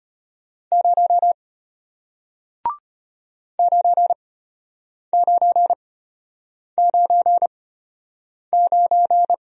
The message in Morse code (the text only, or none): 0 9999